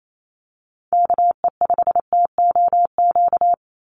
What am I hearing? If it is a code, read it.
Morse code, heard as a keyed tone, about 28 wpm, 700 Hz. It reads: KE5TOQ
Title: USA Call Signs 28wpm